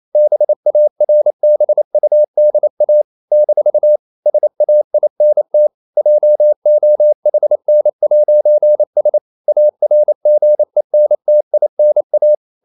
Morse code, heard as a keyed tone, about 28 wpm, 600 Hz. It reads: BARBUDA - SAINT JOHN'S ARGENTINA
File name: CW / Practice / NationalCapitals28